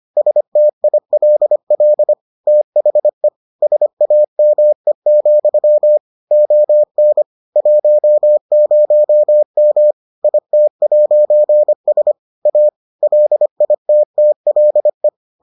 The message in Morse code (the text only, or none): STILL THE SAME, ON 10M IT'S A LITTLE